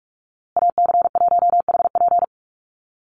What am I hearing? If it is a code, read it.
Morse code, heard as a keyed tone, about 44 wpm, 700 Hz. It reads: AC1HP
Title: USA Call Signs Repeated 2x 45wpm